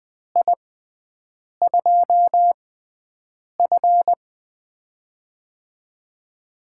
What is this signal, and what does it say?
Morse code, heard as a keyed tone, about 20 wpm, spaced as for 7 wpm, 700 Hz. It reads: I2F
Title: USA Call Signs - 4 Characters - 6x Character Spacing - 20wpm